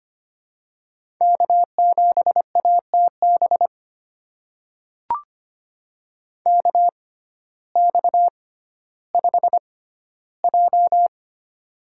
Morse code, heard as a keyed tone, about 25 wpm, spaced as for 14 wpm, 700 Hz. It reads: K7ATB KX5J
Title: USA Call Signs - 6x Character Spacing - 25wpm